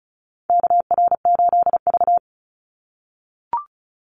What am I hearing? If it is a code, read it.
Morse code, heard as a keyed tone, about 34 wpm, 700 Hz. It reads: KR8V
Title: Call Signs - 35wpm